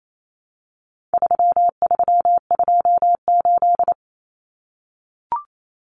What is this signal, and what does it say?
Morse code, heard as a keyed tone, about 28 wpm, 700 Hz. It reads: 3328
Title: Common Cut Numbers - 4-Digit 28wpm